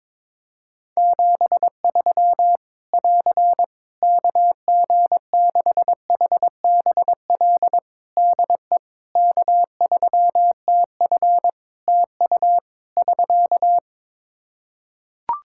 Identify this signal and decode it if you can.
Morse code, heard as a keyed tone, about 22 wpm, 700 Hz. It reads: 73 <AR> KG6HBL DE K3TF TU <SK>
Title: Practice QSO - Exchange 4 - Ending - 22wpm